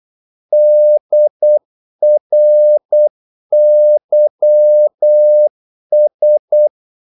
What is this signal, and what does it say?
Morse code, heard as a keyed tone, about 8 wpm, 600 Hz. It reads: DRYS
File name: Korte ord DK 8wpm